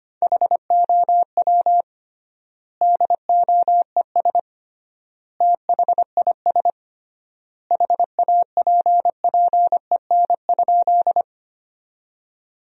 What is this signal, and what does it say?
Morse code, heard as a keyed tone, about 25 wpm, 700 Hz. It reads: HOW DOES THIS HAPPEN?
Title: 3x Word-Spacing - Sentences from Top 400 Words - 25wpm